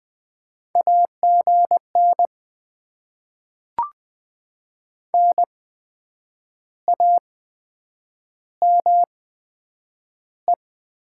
Morse code, heard as a keyed tone, about 20 wpm, 700 Hz. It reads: A G N N A M E
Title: Top 20 QSO Elements - 8x Character Spacing - 20wpm